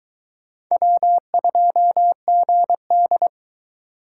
Morse code, heard as a keyed tone, about 22 wpm, 700 Hz. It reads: W2GD